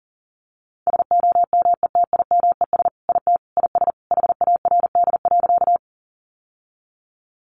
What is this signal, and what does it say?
Morse code, heard as a keyed tone, about 40 wpm, 700 Hz. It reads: SOMETIMES IT IS HARD.
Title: Sentences from Top 600 Words 40wpm